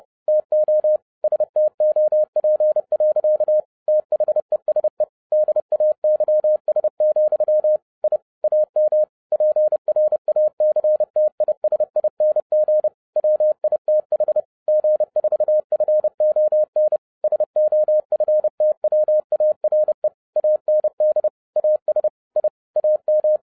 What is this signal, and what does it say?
Morse code, heard as a keyed tone, about 30 wpm, 610 Hz. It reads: TO STOP. THESE DAYS, I AM PRACTISING WITH G4FON SOFTWARE AND AS I AM